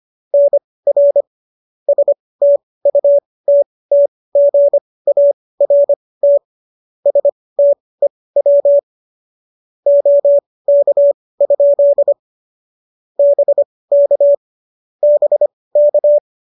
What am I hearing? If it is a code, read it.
Morse code, heard as a keyed tone, about 25 wpm, spaced as for 18 wpm, 590 Hz. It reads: NR STUTTGART STEW OK? BK BK
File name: qso218 18